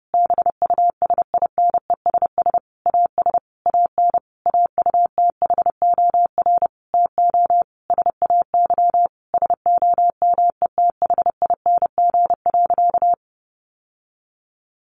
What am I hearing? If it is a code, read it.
Morse code, heard as a keyed tone, about 28 wpm, 700 Hz. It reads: BUSINESS AS AN AUTHOR TO SAY SOMETHING.
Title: Sentences from Top 1200 Words 30wpm